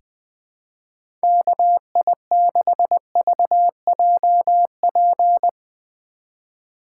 Morse code, heard as a keyed tone, about 20 wpm, 700 Hz. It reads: KI6VJP
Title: USA Call Signs - 6 Characters - 4x Character Spacing - 20wpm